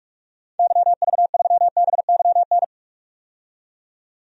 Morse code, heard as a keyed tone, about 46 wpm, 700 Hz. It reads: YV3BYN
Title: DXCC-100 Call Signs - 6 Characters - 45wpm